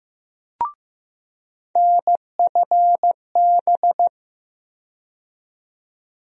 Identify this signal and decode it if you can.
Morse code, heard as a keyed tone, about 15 wpm, 700 Hz. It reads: NFB
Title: USA Call Sign Postfixes - 3 Letters 15wpm